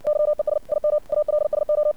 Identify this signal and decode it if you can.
Morse code, heard as a keyed tone, about 59 wpm, 600 Hz. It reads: YES IM ALSO
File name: archived / CRSnet